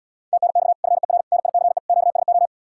Morse code, heard as a keyed tone, about 79 wpm, 700 Hz. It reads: TNX FER NICE QSO